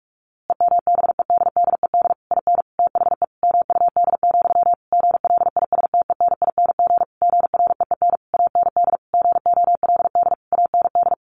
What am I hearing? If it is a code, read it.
Morse code, heard as a keyed tone, about 44 wpm, 700 Hz. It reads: EMBEDDED IN THE MUD, GLISTENING GREEN AND GOLD AND
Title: A Sound Of Thunder 45wpm